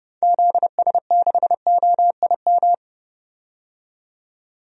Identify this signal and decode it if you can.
Morse code, heard as a keyed tone, about 30 wpm, 700 Hz. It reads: ZS6OIM
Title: DXCC-100 Call Signs - 6 Characters - 30wpm